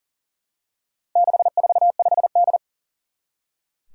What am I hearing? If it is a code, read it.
Morse code, heard as a keyed tone, about 38 wpm, 700 Hz. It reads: 645D
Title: Four Challenging Letter-Number Combinations 40wpm